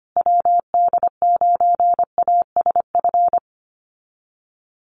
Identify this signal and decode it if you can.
Morse code, heard as a keyed tone, about 25 wpm, 700 Hz. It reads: WD9ASF
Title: USA Call Signs Repeated 2x 25wpm